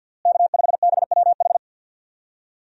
Morse code, heard as a keyed tone, about 53 wpm, 700 Hz. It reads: K5BWH
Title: USA Call Signs - 5 Characters - Repeated 2x - 50wpm